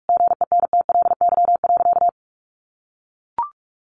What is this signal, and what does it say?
Morse code, heard as a keyed tone, about 44 wpm, 700 Hz. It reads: GENTLY.
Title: Sentences from Top 1500 Words 45wpm